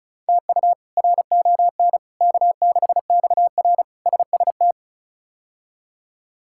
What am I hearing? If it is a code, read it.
Morse code, heard as a keyed tone, about 36 wpm, 700 Hz. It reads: TU RON K6XR SST
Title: K1USN SST QSO Practice - 35wpm